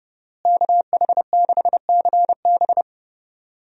Morse code, heard as a keyed tone, about 32 wpm, 700 Hz. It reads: KH6CB
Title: USA Call Signs - 5 Characters - Repeated 2x - 30wpm